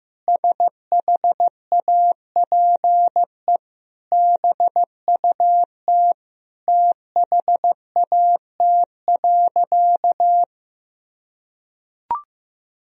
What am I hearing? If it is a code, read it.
Morse code, heard as a keyed tone, about 15 wpm, 700 Hz. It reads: SHAPE BUT THAT.
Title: Sentences from Top 1000 Words - Review Part 3 - 15wpm